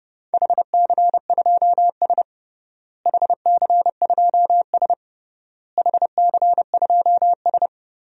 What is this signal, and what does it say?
Morse code, heard as a keyed tone, about 30 wpm, 700 Hz. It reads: HC2S HC2S HC2S
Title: DXCC-100 Call Signs - 4 Characters - Repeated 3x - 30wpm